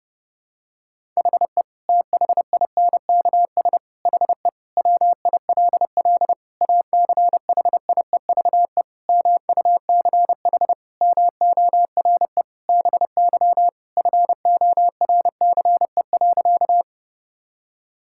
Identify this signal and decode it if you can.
Morse code, heard as a keyed tone, about 30 wpm, 700 Hz. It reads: HE THINKS HE WILL ACHIEVE MUCH MORE BY FORCE.
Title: Sentences from Top 1000 Words - Review Part 6 - 30wpm